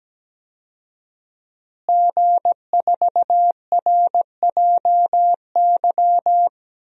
Morse code, heard as a keyed tone, about 17 wpm, 700 Hz. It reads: G4RJY
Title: DXCC-100 Call Signs - 5 Characters - Repeated 3x - 17wpm